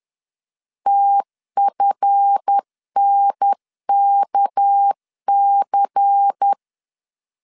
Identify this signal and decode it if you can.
Morse code, heard as a keyed tone, about 10 wpm, 800 Hz. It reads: TFNKC
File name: LEZIONI / 7 GF9Z / 7 GF 10wpm 2514